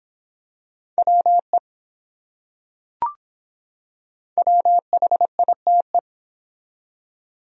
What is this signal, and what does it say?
Morse code, heard as a keyed tone, about 24 wpm, 700 Hz. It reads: WE WHITE